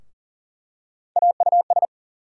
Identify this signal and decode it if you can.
Morse code, heard as a keyed tone, about 40 wpm, 700 Hz. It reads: AUS
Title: USA Call Sign Postfixes - 3 Letters 40wpm